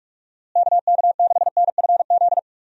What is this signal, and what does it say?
Morse code, heard as a keyed tone, about 46 wpm, 700 Hz. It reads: KK6NFZ
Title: USA Call Signs - 6 Characters - Repeated 2x - 45wpm